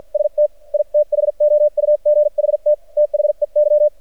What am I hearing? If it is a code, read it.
Morse code, heard as a keyed tone, about 54 wpm, spaced as for 44 wpm, 600 Hz. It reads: ST I THOUGHT THEY
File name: archived / CRSnet